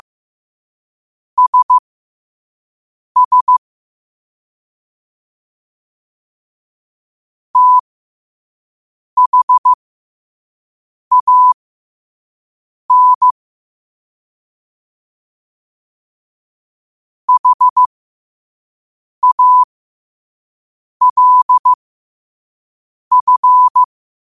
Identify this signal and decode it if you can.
Morse code, heard as a keyed tone, about 15 wpm, spaced as for 5 wpm, 1000 Hz. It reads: SS THAN HALF
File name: qotdfiles / 5wpm